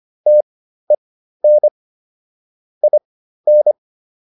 Morse code, heard as a keyed tone, about 25 wpm, spaced as for 13 wpm, 600 Hz. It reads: TEN IN